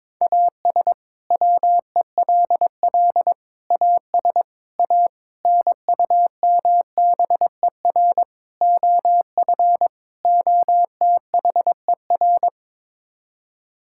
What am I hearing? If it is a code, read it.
Morse code, heard as a keyed tone, about 22 wpm, 700 Hz. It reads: AS WELL AS A NUMBER OF OTHER
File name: practice / 22wpm / Sets of 7 Words - 22wpm